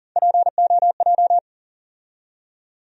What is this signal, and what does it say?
Morse code, heard as a keyed tone, about 40 wpm, 700 Hz. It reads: POJ
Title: USA Call Sign Postfixes - 3 Letters 40wpm